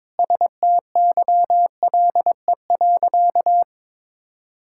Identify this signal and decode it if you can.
Morse code, heard as a keyed tone, about 23 wpm, 700 Hz. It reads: STYLE.
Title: Sentences from Top 1000 Words - Review Part 5 - 22wpm